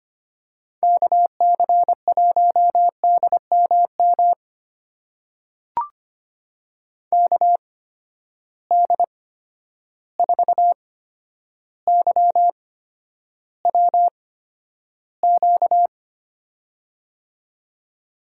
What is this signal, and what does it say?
Morse code, heard as a keyed tone, about 25 wpm, spaced as for 11 wpm, 700 Hz. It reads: KC1DMM KD4YWQ
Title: USA Call Signs - 6 Characters - 8x Character Spacing - 25wpm